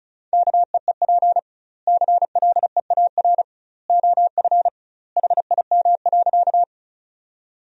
Morse code, heard as a keyed tone, about 36 wpm, 700 Hz. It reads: KEEP CLEAR OF HIM.
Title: 2x Word-Spacing - Sentences from Top 500 Words - 35wpm